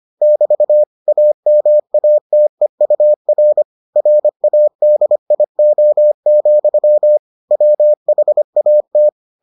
Morse code, heard as a keyed tone, about 25 wpm, 600 Hz. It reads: <BT> AMATEUR RADIO, WHAT